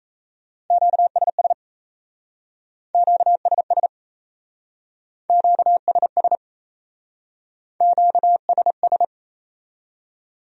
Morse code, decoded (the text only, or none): QSS QSS QSS QSS